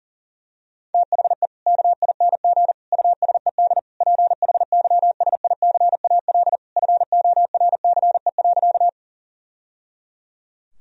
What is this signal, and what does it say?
Morse code, heard as a keyed tone, about 37 wpm, 700 Hz. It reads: THE KING USED PHYSICAL FORCE.
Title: Sentences from Top 1000 Words - Review Part 4 - 40wpm